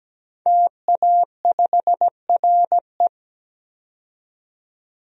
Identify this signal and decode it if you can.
Morse code, heard as a keyed tone, about 17 wpm, 700 Hz. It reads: TA5RE
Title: DXCC-100 Call Signs - 5 Characters - 17wpm